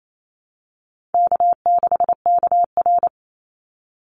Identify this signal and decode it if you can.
Morse code, heard as a keyed tone, about 28 wpm, 700 Hz. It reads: K6KR
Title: Easy Call Signs Repeated 3x - 28wpm